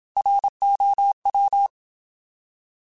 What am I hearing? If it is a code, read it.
Morse code, heard as a keyed tone, about 27 wpm, 800 Hz. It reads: ROW
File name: Alphabetical / Alphabetical mp3s / Rs @25